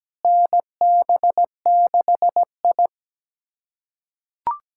Morse code, heard as a keyed tone, about 17 wpm, 700 Hz. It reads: NB6I
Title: USA Callsign DE Callsign - 4 Characters - 17wpm